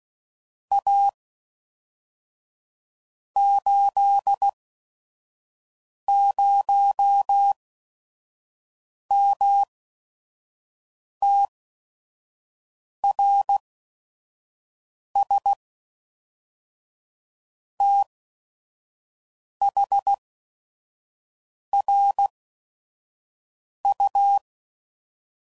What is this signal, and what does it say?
Morse code, heard as a keyed tone, about 16 wpm, spaced as for 5 wpm, 800 Hz. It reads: A 80MTRS THRU
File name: CW Code Groups @5WPM / QSOs / WD0FBT 2